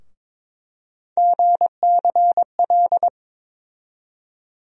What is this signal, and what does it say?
Morse code, heard as a keyed tone, about 22 wpm, 700 Hz. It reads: GCL
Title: USA Call Sign Postfixes - 3 Letters 22wpm